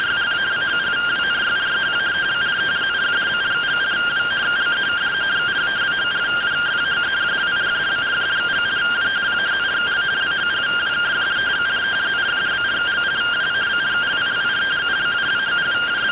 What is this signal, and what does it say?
Two-tone teleprinter signal, about 170 Hz shift.